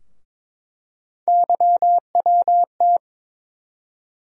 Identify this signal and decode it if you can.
Morse code, heard as a keyed tone, about 22 wpm, 700 Hz. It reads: YWT
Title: USA Call Sign Postfixes - 3 Letters 22wpm